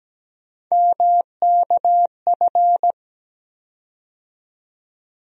Morse code, heard as a keyed tone, about 17 wpm, 700 Hz. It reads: MKF
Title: USA Call Sign Postfixes - 3 Letters 17wpm